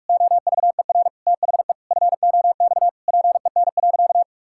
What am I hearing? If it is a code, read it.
Morse code, heard as a keyed tone, about 46 wpm, 700 Hz. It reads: OVER THE FOX PEN.
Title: Pangram Challenge v2 - 45wpm